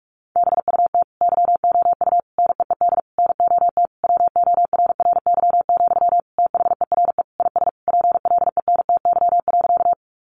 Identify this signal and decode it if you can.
Morse code, heard as a keyed tone, about 44 wpm, 700 Hz. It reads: BUT YOU NEED NOT WORRY, THERE IS PLENTY.
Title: Sentences from Top 1300 Words 45wpm